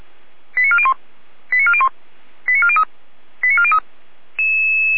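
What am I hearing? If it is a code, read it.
Five-tone selective call (ZVEI), heard as quick runs of five tones, tones of about 70 ms. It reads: 89401 89401 89403 89403
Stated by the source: Five-tone ZVEI selective call received by SDR, fire brigade Orlamünde (feuerwehr-orlamuende).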